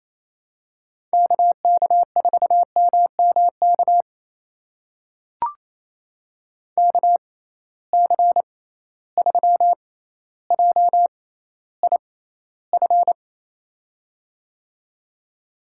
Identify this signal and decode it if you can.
Morse code, heard as a keyed tone, about 28 wpm, spaced as for 14 wpm, 700 Hz. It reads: KK4MMK KC3JIF
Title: USA Call Signs - 6 Characters - 6x Character Spacing - 28wpm